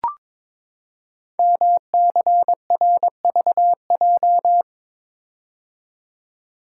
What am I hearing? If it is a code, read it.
Morse code, heard as a keyed tone, about 22 wpm, 700 Hz. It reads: MCRVJ
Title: Code Groups - 5 Letters - 22wpm